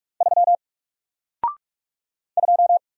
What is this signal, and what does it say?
Morse code, heard as a keyed tone, about 44 wpm, 700 Hz. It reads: 3 2
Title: Single Character (Letter:Number:Punctuation:Prosign) 45wpm